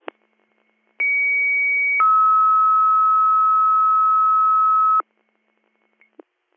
Two-tone page, heard as a one-second tone, then a three-second tone, about 2250 then 1290 Hz. It is heard from 1.0 to 5.0 s.